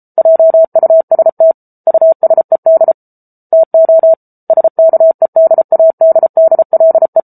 Morse code, heard as a keyed tone, about 34 wpm, 650 Hz. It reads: JUST USED TO SKEDADDLE
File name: Book Two / 07-Chapter 7 - 33 WPM